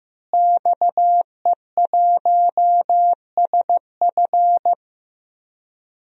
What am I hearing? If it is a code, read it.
Morse code, heard as a keyed tone, about 16 wpm, 700 Hz. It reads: XE1SF
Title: DXCC-100 Call Signs - 5 Characters - Repeated 3x - 15wpm